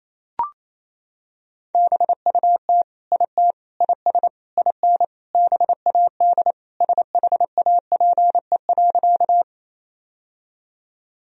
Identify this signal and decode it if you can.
Morse code, heard as a keyed tone, about 28 wpm, 700 Hz. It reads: BUT IT IS IN BAD SHAPE.
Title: Sentences from Top 1000 Words - Review Part 3 - 28wpm